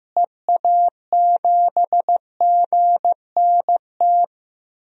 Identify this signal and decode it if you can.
Morse code, heard as a keyed tone, about 15 wpm, 700 Hz. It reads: EA7GNT
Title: DXCC-100 Call Signs - 6 Characters - Repeated 3x - 15wpm